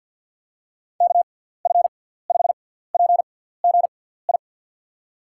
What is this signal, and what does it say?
Morse code, heard as a keyed tone, about 48 wpm, spaced as for 17 wpm, 700 Hz. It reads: KF5PGI
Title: USA Call Signs - 6 Characters - 6x Character Spacing - 50wpm